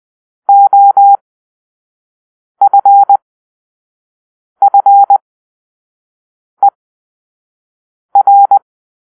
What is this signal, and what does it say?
Morse code, heard as a keyed tone, about 20 wpm, spaced as for 5 wpm, 800 Hz. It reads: OFFER
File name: arrl / Grp-5 Drill-4